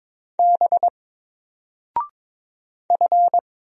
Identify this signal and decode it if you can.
Morse code, heard as a keyed tone, about 22 wpm, 700 Hz. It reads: B F
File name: Single-Letter-Number-22wpm